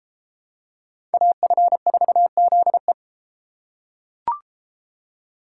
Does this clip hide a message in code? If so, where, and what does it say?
Yes: Morse code from 1.1 to 2.9 s, heard as a keyed tone, about 33 wpm, 700 Hz. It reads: AF4ZE